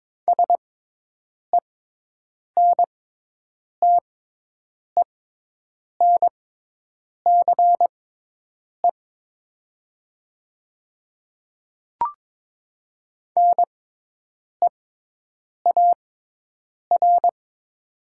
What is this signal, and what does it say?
Morse code, heard as a keyed tone, about 22 wpm, spaced as for 8 wpm, 700 Hz. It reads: SENTENCE NEAR
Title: 6x Character Spacing - Top 200 Words - 22wpm